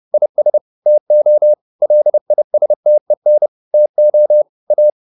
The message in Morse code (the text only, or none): IS TO LISTEN TO A